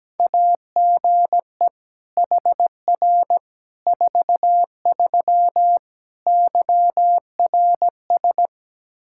Morse code, heard as a keyed tone, about 17 wpm, 700 Hz. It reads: AGE HR 43 YRS
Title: Practice QSO - Exchange 3 - More Info - 17wpm